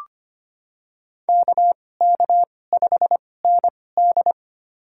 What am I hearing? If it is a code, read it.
Morse code, heard as a keyed tone, about 25 wpm, spaced as for 18 wpm, 700 Hz. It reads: KK5ND
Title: USA Call Signs - 5 Characters - 2x Character Spacing - 25wpm